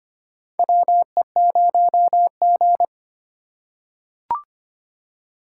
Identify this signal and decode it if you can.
Morse code, heard as a keyed tone, about 25 wpm, 700 Hz. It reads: WE0G
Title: USA Call Signs - 4 Characters - 8x Character Spacing - 25wpm